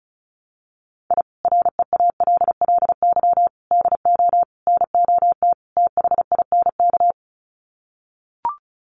Morse code, heard as a keyed tone, about 34 wpm, 700 Hz. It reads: I REALLY DO NOT THINK
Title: Sets of 5 Words Encore2 35wpm